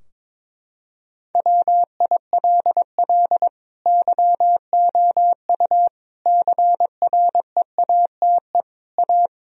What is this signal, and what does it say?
Morse code, heard as a keyed tone, about 22 wpm, 700 Hz. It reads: WILL YOU CREATE A
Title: Sentences from Top 1000 Words - Review Part 3 - 22wpm